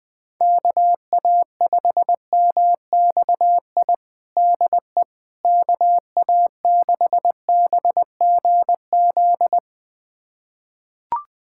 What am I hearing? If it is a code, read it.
Morse code, heard as a keyed tone, about 21 wpm, 700 Hz. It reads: KA5MXI DE KA6BGZ
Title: USA Callsign DE Callsign - 6 Characters - 20wpm